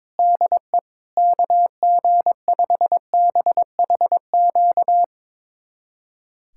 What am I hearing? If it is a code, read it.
Morse code, heard as a keyed tone, about 21 wpm, 700 Hz. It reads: DE KG5BHQ
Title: USA Callsign DE Callsign 22wpm